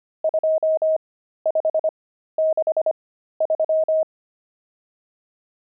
Morse code, heard as a keyed tone, about 25 wpm, spaced as for 13 wpm, 630 Hz. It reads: 2563
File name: pangram-fs-13-cs-25-nums13